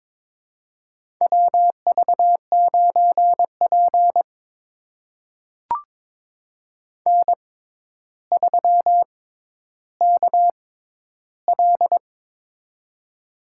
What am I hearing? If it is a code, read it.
Morse code, heard as a keyed tone, about 23 wpm, spaced as for 11 wpm, 700 Hz. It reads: WV9P N3KL
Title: USA Call Signs - 4 Characters - 6x Character Spacing - 22wpm